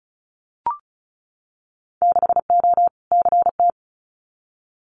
Morse code, heard as a keyed tone, about 34 wpm, 700 Hz. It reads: 6O CT
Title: Winter Field Day Exchanges Encore 35wpm